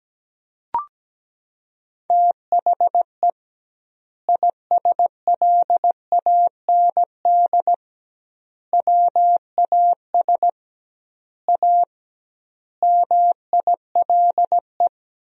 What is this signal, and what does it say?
Morse code, heard as a keyed tone, about 17 wpm, 700 Hz. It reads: THE ISLAND WAS A MILE
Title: 2x Word-Spacing - Sentences from Top 500 Words - 17wpm